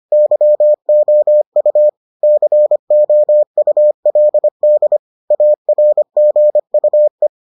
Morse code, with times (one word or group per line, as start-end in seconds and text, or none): 0.1-1.9 YOU
2.2-5.0 COULD
5.3-7.3 ARGUE